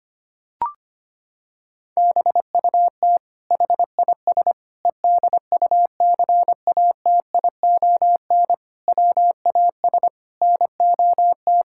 Morse code, heard as a keyed tone, about 25 wpm, 700 Hz. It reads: BUT HIS EDUCATION WAS NOT
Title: Sentences from Top 1000 Words - Review Part 6 - 25wpm